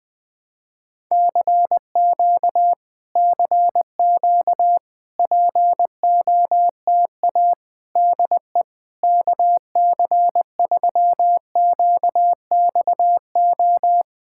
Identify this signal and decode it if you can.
Morse code, heard as a keyed tone, about 20 wpm, 690 Hz. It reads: CQ CQ POTA DE KC3QXO
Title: POTA QSO Practice - 20wpm